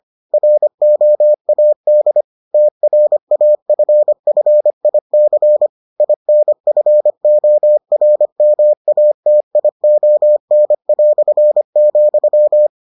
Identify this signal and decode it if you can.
Morse code, heard as a keyed tone, about 25 wpm, 600 Hz. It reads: ROAD TRAFFIC INFORMATION",